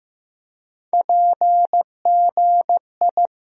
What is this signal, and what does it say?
Morse code, heard as a keyed tone, about 15 wpm, 700 Hz. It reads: PGI
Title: USA Call Sign Postfixes - 3 Letters 15wpm